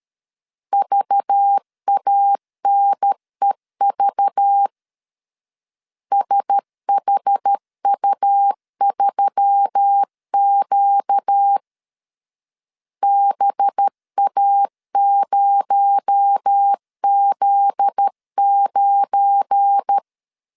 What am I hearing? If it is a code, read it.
Morse code, heard as a keyed tone, about 13 wpm, 800 Hz. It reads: VANEV SHU3Q BA0Z9
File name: CW / DET / 1 9 1500C S B 12wpm 3819